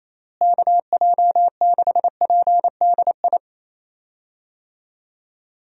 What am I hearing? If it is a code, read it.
Morse code, heard as a keyed tone, about 28 wpm, 700 Hz. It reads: KJ6PDI
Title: USA Call Signs - 6 Characters - Repeated 3x - 28wpm